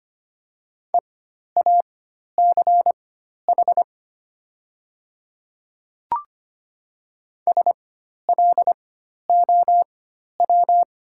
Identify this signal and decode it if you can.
Morse code, heard as a keyed tone, about 25 wpm, spaced as for 12 wpm, 690 Hz. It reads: EACH SLOW
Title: 4x Character Spacing - Top 500 Words - 25wpm